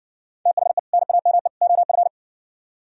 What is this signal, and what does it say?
Morse code, heard as a keyed tone, about 60 wpm, 700 Hz. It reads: THE NAME OF